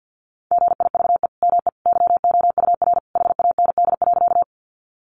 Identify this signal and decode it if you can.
Morse code, heard as a keyed tone, about 50 wpm, 700 Hz. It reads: GIVE ME YOUR HAND.
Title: Sentences from Top 200 Words 50wpm